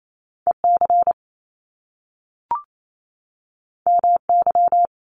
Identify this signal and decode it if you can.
Morse code, heard as a keyed tone, about 28 wpm, 700 Hz. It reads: E C M Y
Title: Top Two Letter Combinations in Words 28wpm